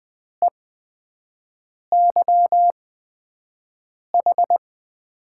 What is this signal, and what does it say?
Morse code, heard as a keyed tone, about 21 wpm, spaced as for 5 wpm, 700 Hz. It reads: EYH
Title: USA Call Signs - 5 Characters - 8x Character Spacing - 20wpm